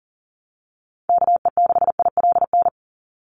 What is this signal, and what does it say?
Morse code, heard as a keyed tone, about 40 wpm, 700 Hz. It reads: KE6ILN
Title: USA Call Signs 40wpm